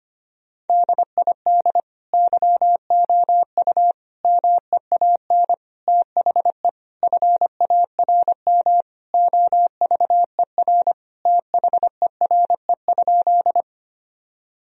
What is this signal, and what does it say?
Morse code, heard as a keyed tone, about 25 wpm, 700 Hz. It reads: DID YOU MEAN THE FARM OVER THERE?
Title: Sentences from Top 1000 Words - Review Part 2 - 25wpm